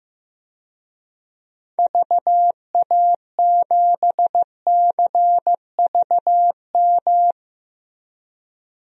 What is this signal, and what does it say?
Morse code, heard as a keyed tone, about 15 wpm, 700 Hz. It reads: VA7CVM
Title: DXCC-100 Call Signs - 6 Characters - Repeated 2x - 15wpm